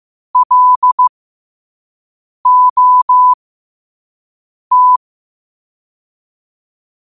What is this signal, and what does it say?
Morse code, heard as a keyed tone, about 15 wpm, spaced as for 5 wpm, 1000 Hz. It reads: LOT
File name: qotdfiles / 5wpm